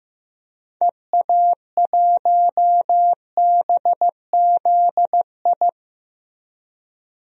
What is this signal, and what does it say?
Morse code, heard as a keyed tone, about 15 wpm, 700 Hz. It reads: EA1BZI
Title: DXCC-100 Call Signs - 6 Characters - Repeated 3x - 15wpm